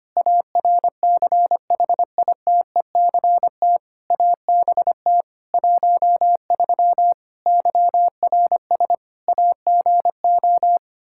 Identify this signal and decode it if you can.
Morse code, heard as a keyed tone, about 25 wpm, 700 Hz. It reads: ARCHITECT ABT 13 YRS AGO
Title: Practice QSO - Exchange 3 - More Info - 25wpm